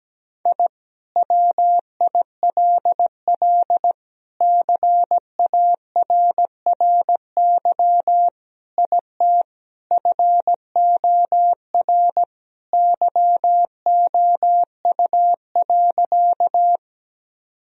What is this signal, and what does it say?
Morse code, heard as a keyed tone, about 17 wpm, 700 Hz. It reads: I WILL CARRY IT FOR YOU.
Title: Sentences from Top 1000 Words - Review Part 1 - 17wpm